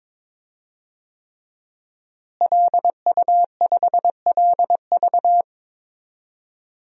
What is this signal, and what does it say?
Morse code, heard as a keyed tone, about 22 wpm, 700 Hz. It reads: LU5LV
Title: DXCC-100 Call Signs - 5 Characters - Repeated 3x - 22wpm